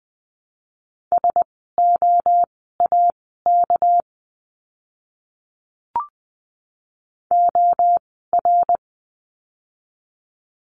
Morse code, heard as a keyed tone, about 20 wpm, spaced as for 14 wpm, 700 Hz. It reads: SOAK OR
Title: 199 Lesson 32f - Letter k - Words 20wpm